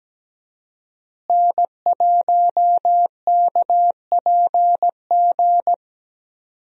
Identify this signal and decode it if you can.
Morse code, heard as a keyed tone, about 17 wpm, 700 Hz. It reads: N1KPG
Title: USA Call Signs - 5 Characters - Repeated 2x - 17wpm